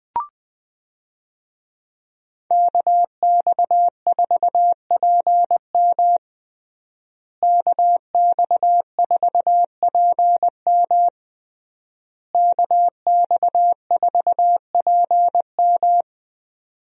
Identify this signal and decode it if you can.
Morse code, heard as a keyed tone, about 20 wpm, 700 Hz. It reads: KX4PM KX4PM KX4PM
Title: USA Call Signs - 5 Characters - Repeated 3x - 20wpm